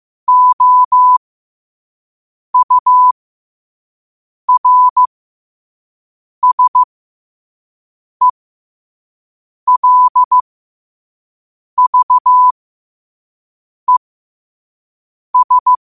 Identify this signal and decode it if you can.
Morse code, heard as a keyed tone, about 15 wpm, spaced as for 5 wpm, 1000 Hz. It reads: OURSELVES
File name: qotdfiles / 5wpm